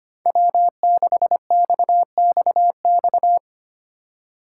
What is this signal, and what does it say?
Morse code, heard as a keyed tone, about 25 wpm, 700 Hz. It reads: W6XXX
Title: USA Call Signs - 5 Characters - 2x Character Spacing - 25wpm